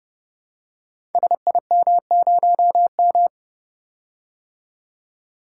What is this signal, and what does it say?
Morse code, heard as a keyed tone, about 30 wpm, 700 Hz. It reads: SIM0M
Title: Code Groups - 5 Letter:Numbers - 30wpm